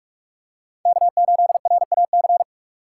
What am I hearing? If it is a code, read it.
Morse code, heard as a keyed tone, about 43 wpm, 700 Hz. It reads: K8RAC